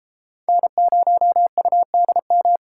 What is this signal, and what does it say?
Morse code, heard as a keyed tone, about 33 wpm, 700 Hz. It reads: N0UDM